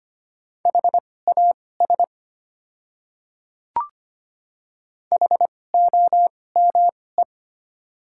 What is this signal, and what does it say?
Morse code, heard as a keyed tone, about 25 wpm, spaced as for 18 wpm, 690 Hz. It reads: HAS HOME
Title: 2x Character Spacing - Top 200 Words - 25wpm